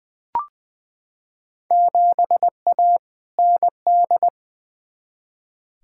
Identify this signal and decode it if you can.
Morse code, heard as a keyed tone, about 20 wpm, 700 Hz. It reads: 7A ND
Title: Field Day Exchanges Encore - 20wpm